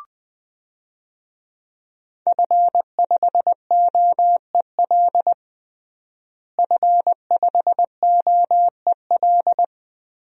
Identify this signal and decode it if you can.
Morse code, heard as a keyed tone, about 20 wpm, 700 Hz. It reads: F5OEL F5OEL
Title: DXCC-100 Call Signs - 5 Characters - Repeated 2x - 20wpm